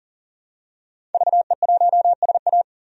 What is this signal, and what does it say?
Morse code, heard as a keyed tone, about 37 wpm, 700 Hz. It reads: VE1SA
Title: DXCC-100 Call Signs - 5 Characters - 40wpm